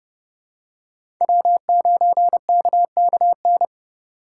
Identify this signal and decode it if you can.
Morse code, heard as a keyed tone, about 30 wpm, 700 Hz. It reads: W9KKN
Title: USA Call Signs - 4x Character Spacing - 30wpm